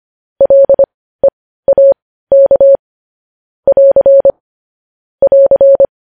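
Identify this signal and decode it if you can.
Morse code, heard as a keyed tone, about 25 wpm, spaced as for 15 wpm, 550 Hz. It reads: LEAK <AR> <AR>
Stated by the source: Morse news headlines practice at 15 wpm.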